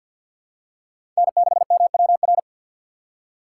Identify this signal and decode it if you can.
Morse code, heard as a keyed tone, about 46 wpm, 690 Hz. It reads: N6MWR
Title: USA Call Signs - 5 Characters - 6x Character Spacing - 50wpm